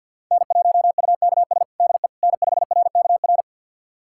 Morse code, heard as a keyed tone, about 52 wpm, 690 Hz. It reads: N1UKS DE N5RKR
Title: USA Callsign DE Callsign - 5 Characters - 50wpm